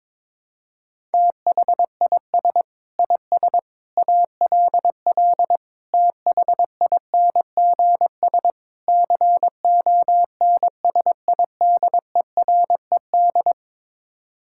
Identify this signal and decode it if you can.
Morse code, heard as a keyed tone, about 22 wpm, 700 Hz. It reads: THIS IS ALL THINGS CONSIDERED
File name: practice / 22wpm / Build A Phrase - 5-Word Phrases - 22wpm